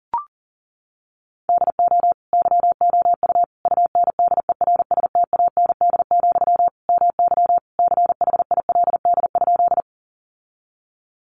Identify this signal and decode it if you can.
Morse code, heard as a keyed tone, about 40 wpm, 700 Hz. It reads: DO YOU UNDERSTAND, MY CHILD?
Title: Sentences from Top 800 Words 40wpm